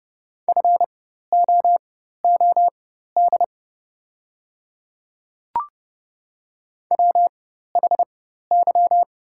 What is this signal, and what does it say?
Morse code, heard as a keyed tone, about 30 wpm, spaced as for 14 wpm, 700 Hz. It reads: FOOD WHY
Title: 4x Character Spacing - Top 300 Words - 30wpm